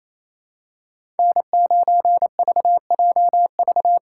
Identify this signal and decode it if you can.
Morse code, heard as a keyed tone, about 28 wpm, 700 Hz. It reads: N9VJV
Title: USA Call Signs - 2x Character Spacing - 28wpm